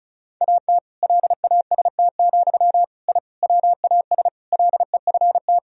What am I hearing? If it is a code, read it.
Morse code, heard as a keyed tone, about 36 wpm, 700 Hz. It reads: AT LAST, I WAS LEFT
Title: Sentences from Top 1000 Words - Review Part 1 - 35wpm